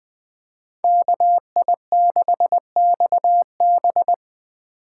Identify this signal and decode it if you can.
Morse code, heard as a keyed tone, about 24 wpm, 700 Hz. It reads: KI6XB
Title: Speed Racing - USA 2x2 Call Signs 20wpm